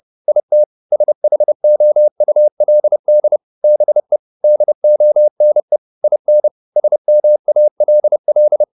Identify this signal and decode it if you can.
Morse code, heard as a keyed tone, about 30 wpm, 600 Hz. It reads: IT SHOULD BE DONE IN SMALL